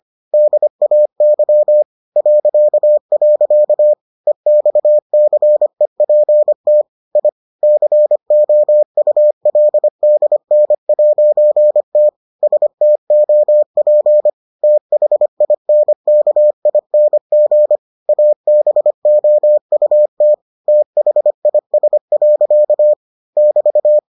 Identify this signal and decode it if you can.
Morse code, heard as a keyed tone, about 25 wpm, 600 Hz. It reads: DAY .. EXCEPT I COULDN'T STOP THINKING ABOUT THIS. <BT>